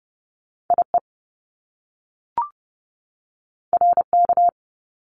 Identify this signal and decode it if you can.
Morse code, heard as a keyed tone, about 30 wpm, 700 Hz. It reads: IE RK
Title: Top Two Letter Combinations in Words 30wpm